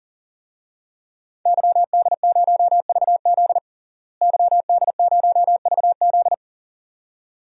Morse code, heard as a keyed tone, about 40 wpm, 700 Hz. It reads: YD0VZ YD0VZ
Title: DXCC-100 Call Signs - 5 Characters - Repeated 2x - 40wpm